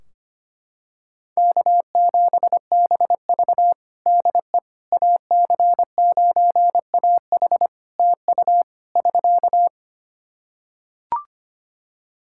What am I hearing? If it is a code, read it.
Morse code, heard as a keyed tone, about 25 wpm, 700 Hz. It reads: K7BV DE AC9AH TU <SK>
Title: Practice QSO - Exchange 4 - Ending - 25wpm